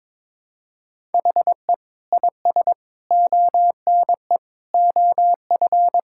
Morse code, heard as a keyed tone, about 22 wpm, 700 Hz. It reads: HE IS ONE OF
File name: practice / 22wpm / Sentences with 4-Word Phrases - 22wpm